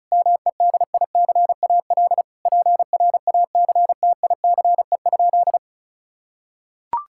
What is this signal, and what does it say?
Morse code, heard as a keyed tone, about 36 wpm, 700 Hz. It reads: MEDICAL PRACTICE?
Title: Sentences from Top 1000 Words - Review Part 5 - 35wpm